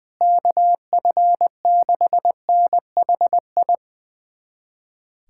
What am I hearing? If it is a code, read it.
Morse code, heard as a keyed tone, about 20 wpm, 700 Hz. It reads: KF6NHI
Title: USA Callsign DE Callsign 20wpm